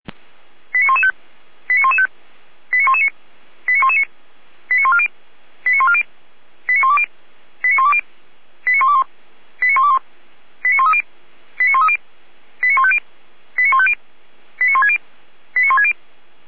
Five-tone selective call (ZVEI), heard as quick runs of five tones, tones of about 70 ms. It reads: 89115 89115 89119 89119 89140 89140 89120 89120 89121 89121 89130 89130 89150 89150 89160 89160
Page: ZVEI